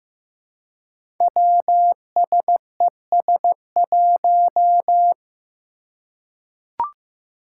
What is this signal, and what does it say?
Morse code, heard as a keyed tone, about 15 wpm, 700 Hz. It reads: WSES1
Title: Five Challenging Letter-Number Combinations 15wpm